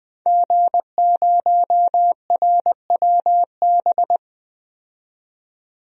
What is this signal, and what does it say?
Morse code, heard as a keyed tone, about 20 wpm, 700 Hz. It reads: G0RWB
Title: DXCC-100 Call Signs - 5 Characters - 20wpm